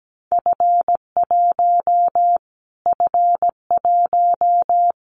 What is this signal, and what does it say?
Morse code, heard as a keyed tone, about 17 wpm, 700 Hz. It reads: F1 F1
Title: Callsign Prefixes for Top 25 DXCC - Repeated 3x - 17wpm